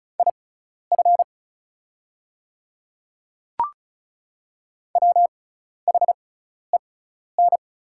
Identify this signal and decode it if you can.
Morse code, heard as a keyed tone, about 34 wpm, spaced as for 12 wpm, 690 Hz. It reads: IF WHEN
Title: 6x Character Spacing - Top 100 Words - 35wpm